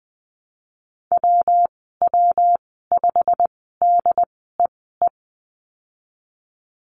Morse code, heard as a keyed tone, about 20 wpm, spaced as for 15 wpm, 700 Hz. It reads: WW5DEE
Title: 128 Lesson 21g - Number 6 - Call Signs 20wpm